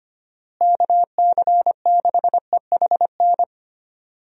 Morse code, heard as a keyed tone, about 25 wpm, 700 Hz. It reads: KC6EHN
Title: USA Call Signs - 6 Characters - Repeated 3x - 25wpm